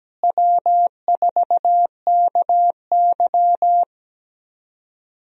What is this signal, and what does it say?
Morse code, heard as a keyed tone, about 17 wpm, 700 Hz. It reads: W4KY
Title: USA Call Signs - 4 Characters - Repeated 2x - 17wpm